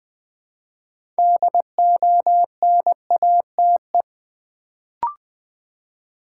Morse code, heard as a keyed tone, about 20 wpm, 700 Hz. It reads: DONATE
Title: Build A Word - 3 Letters At A Time - 6-Letter Words - 20wpm